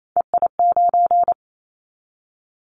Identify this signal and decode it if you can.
Morse code, heard as a keyed tone, about 28 wpm, 700 Hz. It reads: EI9
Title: Callsign Prefixes for Top 50 DXCC - 28wpm